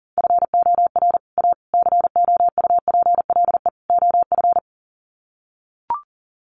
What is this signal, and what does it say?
Morse code, heard as a keyed tone, about 40 wpm, 700 Hz. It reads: FOR A COUPLE OF
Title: Sets of 4 Words 40wpm